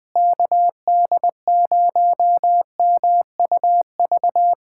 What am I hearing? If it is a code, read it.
Morse code, heard as a keyed tone, about 19 wpm, 700 Hz. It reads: KD0MUV
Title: USA Callsign DE Callsign 20wpm